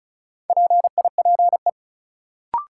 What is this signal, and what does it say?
Morse code, heard as a keyed tone, about 34 wpm, 700 Hz. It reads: PIPE
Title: Sentences from Top 1500 Words - New Word Drill - 35wpm